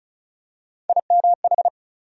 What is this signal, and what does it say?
Morse code, heard as a keyed tone, about 36 wpm, 700 Hz. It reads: IMH
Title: USA Call Sign Postfixes - 3 Letters 35wpm